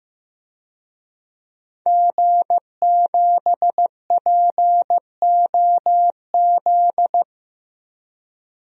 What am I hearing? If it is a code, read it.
Morse code, heard as a keyed tone, about 15 wpm, 700 Hz. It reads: G7POZ
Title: DXCC-100 Call Signs - 5 Characters - Repeated 3x - 15wpm